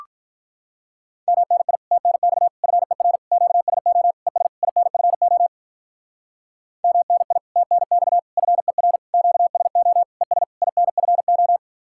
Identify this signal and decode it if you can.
Morse code, heard as a keyed tone, about 43 wpm, 700 Hz. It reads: MNI TNX FER QSO ES INFO MNI TNX FER QSO ES INFO
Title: Speed-Racing Common QSO Phrases - 35wpm